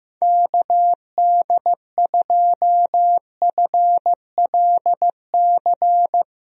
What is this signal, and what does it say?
Morse code, heard as a keyed tone, about 15 wpm, 700 Hz. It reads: KD2FLC
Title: USA Call Signs - 6 Characters - Repeated 2x - 15wpm